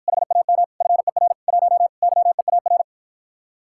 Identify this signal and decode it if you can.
Morse code, heard as a keyed tone, about 52 wpm, 700 Hz. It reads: HAM FER 1 YEAR